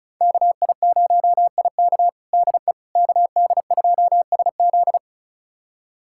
Morse code, heard as a keyed tone, about 36 wpm, 700 Hz. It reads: KI0IK DE KD2SZ
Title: USA Callsign DE Callsign - 5 Characters - 35wpm